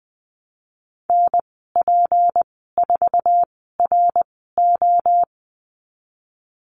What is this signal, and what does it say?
Morse code, heard as a keyed tone, about 20 wpm, spaced as for 15 wpm, 700 Hz. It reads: NP4RO
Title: 186 Lesson 30g - Letter b - Call Signs 20wpm